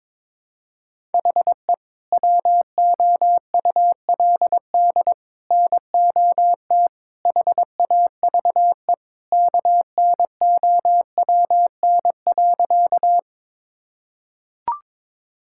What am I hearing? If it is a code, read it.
Morse code, heard as a keyed tone, about 22 wpm, 700 Hz. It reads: HE WOULD NOT HAVE KNOWN.
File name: practice / 22wpm / Sentences with 3-Word Phrases - 22wpm